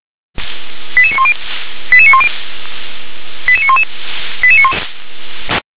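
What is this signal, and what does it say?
Five-tone selective call (ZVEI), heard as quick runs of five tones, tones of about 70 ms. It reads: 88910 88910 88911
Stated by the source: ZVEI five-tone selective call received by SDR.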